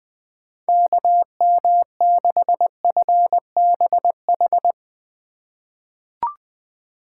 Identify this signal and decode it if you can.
Morse code, heard as a keyed tone, about 20 wpm, 700 Hz. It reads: KM6FBH
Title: USA Call Signs - 6 Characters - 2x Character Spacing - 20wpm